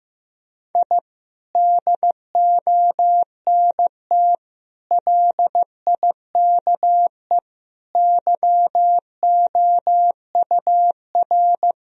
Morse code, heard as a keyed tone, about 15 wpm, 700 Hz. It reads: I DONT LIKE YOUR
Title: Sentences from Top 1000 Words - Review Part 3 - 15wpm